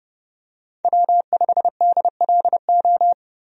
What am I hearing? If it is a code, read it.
Morse code, heard as a keyed tone, about 30 wpm, 700 Hz. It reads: W5DLO
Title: USA Call Signs - 5 Characters - 30wpm